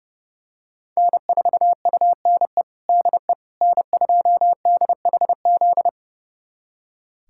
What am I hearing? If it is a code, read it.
Morse code, heard as a keyed tone, about 30 wpm, 700 Hz. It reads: N4UNE DE N2DHZ
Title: USA Callsign DE Callsign - 5 Characters - 30wpm